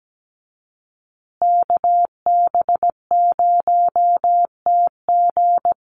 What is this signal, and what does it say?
Morse code, heard as a keyed tone, about 17 wpm, 700 Hz. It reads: KB0TG
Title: USA Call Signs Repeated 3x 17wpm